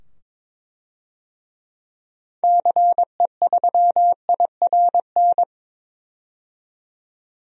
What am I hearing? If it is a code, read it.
Morse code, heard as a keyed tone, about 22 wpm, 700 Hz. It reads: CE3IRN
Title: DXCC-100 Call Signs - 6 Characters - 22wpm